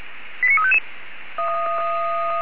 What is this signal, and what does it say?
Five-tone selective call (ZVEI), heard as quick runs of five tones, tones of about 70 ms. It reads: 89355